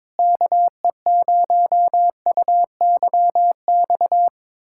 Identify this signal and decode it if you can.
Morse code, heard as a keyed tone, about 22 wpm, 700 Hz. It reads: KE0UYX
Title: USA Callsign DE Callsign - 6 Characters - 22wpm